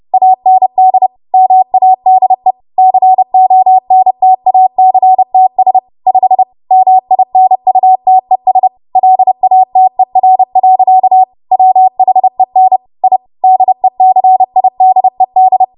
Morse code, heard as a keyed tone, about 30 wpm, 750 Hz. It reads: AND MADE CONTACTS 5 MINUTES LATER. WHEN I DECIDED
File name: Morse / 30WPM